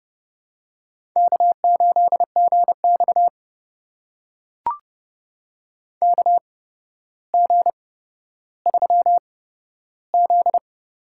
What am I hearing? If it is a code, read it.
Morse code, heard as a keyed tone, about 30 wpm, spaced as for 13 wpm, 700 Hz. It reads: K8GX KG3Z
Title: USA Call Signs - 8x Character Spacing - 30wpm